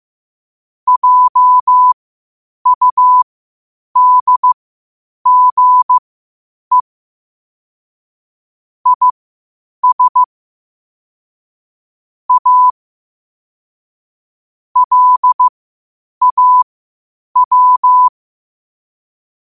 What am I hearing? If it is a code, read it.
Morse code, heard as a keyed tone, about 15 wpm, spaced as for 8 wpm, 1000 Hz. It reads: JUDGE IS A LAW
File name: qotdfiles / 8wpm